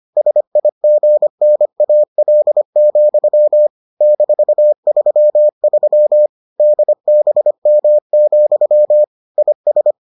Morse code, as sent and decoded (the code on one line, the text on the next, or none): ... .. --. -. .- .-.. --..-- / -....- ...-- ...-- / -.. -... -- --..-- / .. ...
SIGNAL, -33 DBM, IS